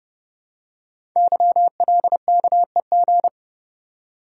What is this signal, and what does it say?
Morse code, heard as a keyed tone, about 30 wpm, 700 Hz. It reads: YLKEG
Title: Code Groups - 5 Letters - 30wpm